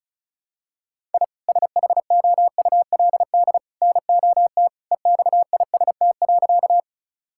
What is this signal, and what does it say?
Morse code, heard as a keyed tone, about 36 wpm, 700 Hz. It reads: I SHOULD NOT EXIST.
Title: Sentences from Top 1000 Words - Review Part 5 - 35wpm